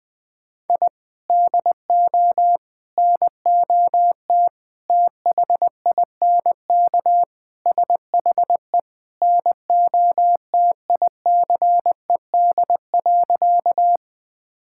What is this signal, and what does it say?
Morse code, heard as a keyed tone, about 20 wpm, 700 Hz. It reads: I DO NOT THINK SHE NOTICED.
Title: Sentences with 5-Word Phrases - 20wpm